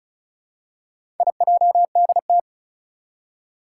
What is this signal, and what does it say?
Morse code, heard as a keyed tone, about 36 wpm, 700 Hz. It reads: IJDT
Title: Four Challenging Letter-Number Combinations 35wpm